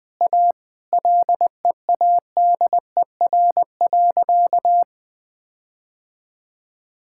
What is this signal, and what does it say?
Morse code, heard as a keyed tone, about 20 wpm, 700 Hz. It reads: A LEADER.
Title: Sentences with 6-Word Phrases - 20wpm